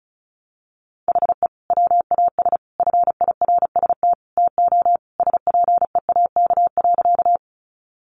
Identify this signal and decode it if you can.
Morse code, heard as a keyed tone, about 34 wpm, 700 Hz. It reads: HE WAS FIRST TO SPEAK.
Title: Sentences from Top 1100 Words 35wpm